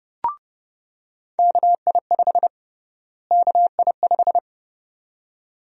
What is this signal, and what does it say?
Morse code, heard as a keyed tone, about 29 wpm, 700 Hz. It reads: KI5 KI5
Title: USA Call Sign Prefixes - Repeated 2x - 30wpm